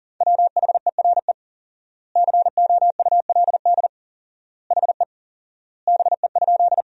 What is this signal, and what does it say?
Morse code, heard as a keyed tone, about 38 wpm, 700 Hz. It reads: WHERE COULD HE BE?
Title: 4x Word-Spacing - Sentences from Top 200 Words - 40wpm